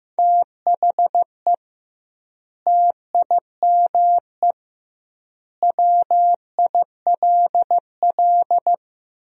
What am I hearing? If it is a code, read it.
Morse code, heard as a keyed tone, about 15 wpm, 700 Hz. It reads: THE TIME WILL
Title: 2x Word-Spacing - Sentences from Top 100 Words - 15wpm